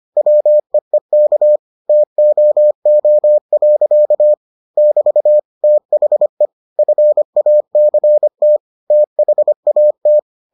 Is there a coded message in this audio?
Yes: Morse code from 0.2 to 10.2 s, heard as a keyed tone, about 25 wpm, 600 Hz. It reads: WEEK TOO. <BT> THE FACT THAT